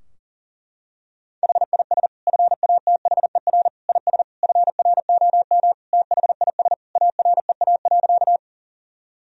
Morse code, heard as a keyed tone, about 40 wpm, 700 Hz. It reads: HIS FATHER IS FROM THIS AREA.
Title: Sentences from Top 1000 Words - Review Part 2 - 40wpm